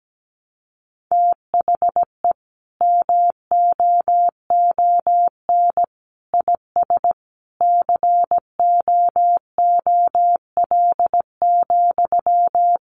Morse code, heard as a keyed tone, about 17 wpm, 700 Hz. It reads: THE MOON IS COOL,
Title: Sentences from Top 1200 Words 17wpm